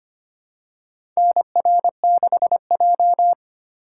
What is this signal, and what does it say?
Morse code, heard as a keyed tone, about 25 wpm, 700 Hz. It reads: NR6J
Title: USA Call Signs - 4 Characters - 3x Character Spacing - 25wpm